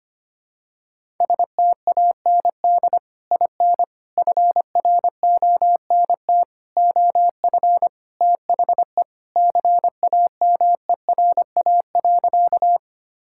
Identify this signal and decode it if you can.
Morse code, heard as a keyed tone, about 25 wpm, 700 Hz. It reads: STAND IN FRONT OF THE CAMERA.
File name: practice / 25wpm / Sentences with 3-Word Phrases - 25wpm